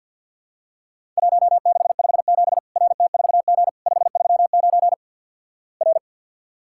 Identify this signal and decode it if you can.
Morse code, heard as a keyed tone, about 50 wpm, 700 Hz. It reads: 1657 RT4G 539 R
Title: Worked All Europe DX Contest - QTC Exchange - 50wpm